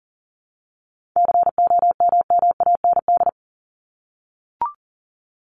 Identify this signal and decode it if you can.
Morse code, heard as a keyed tone, about 40 wpm, 700 Hz. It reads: COMMAND
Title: Top 1501-1750 Words 40wpm